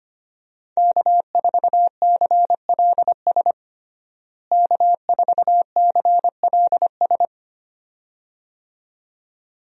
Morse code, heard as a keyed tone, about 25 wpm, 700 Hz. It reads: K4CLS K4CLS
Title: USA Call Signs - 5 Characters - Repeated 2x - 25wpm